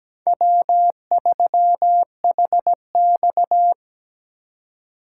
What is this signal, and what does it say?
Morse code, heard as a keyed tone, about 20 wpm, 700 Hz. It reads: W3HX
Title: Speed Racing - USA 1x2 Call Signs 17wpm